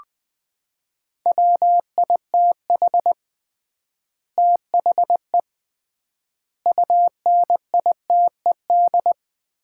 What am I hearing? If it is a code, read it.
Morse code, heard as a keyed tone, about 20 wpm, 700 Hz. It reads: WITH THE UNITED
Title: 3x Word-Spacing - Sets of 4 Words - 20wpm